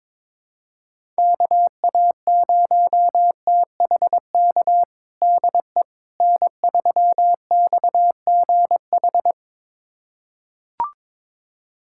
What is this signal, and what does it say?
Morse code, heard as a keyed tone, about 21 wpm, 700 Hz. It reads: KA0THK DE N3XGH
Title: USA Callsign DE Callsign 22wpm